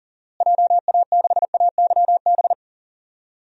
Morse code, heard as a keyed tone, about 40 wpm, 700 Hz. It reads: JA6AYB
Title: DXCC-100 Call Signs - 6 Characters - Repeated 2x - 40wpm